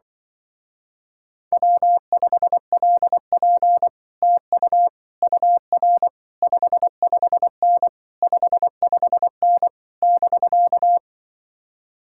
Morse code, heard as a keyed tone, about 25 wpm, 700 Hz. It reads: W5LP TU UR 55N 55N <BK>
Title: Speed-Racing Common QSO Phrases - 20wpm